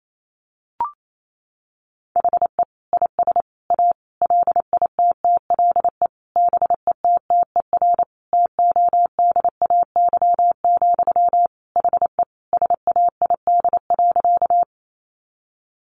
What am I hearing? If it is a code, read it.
Morse code, heard as a keyed tone, about 28 wpm, 700 Hz. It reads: HE IS A LITTLE BETTER TODAY, HE SAID.
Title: Sentences from Top 600 Words 28wpm